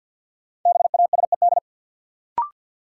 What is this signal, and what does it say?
Morse code, heard as a keyed tone, about 47 wpm, 700 Hz. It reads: BASED
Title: Sets of 2 Words - New Word Drill - 50wpm